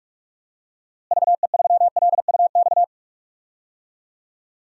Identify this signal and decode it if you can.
Morse code, heard as a keyed tone, about 47 wpm, 700 Hz. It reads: VE3LUX
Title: DXCC-100 Call Signs - 6 Characters - Repeated 3x - 45wpm